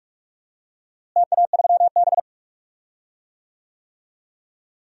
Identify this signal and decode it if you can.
Morse code, heard as a keyed tone, about 41 wpm, 700 Hz. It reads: TA3B
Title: DXCC-100 Call Signs - 4 Characters - Repeated 3x - 45wpm